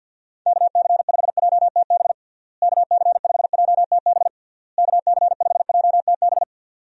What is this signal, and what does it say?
Morse code, heard as a keyed tone, about 51 wpm, 700 Hz. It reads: KC5JTB KC5JTB KC5JTB
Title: USA Call Signs - 6 Characters - Repeated 3x - 50wpm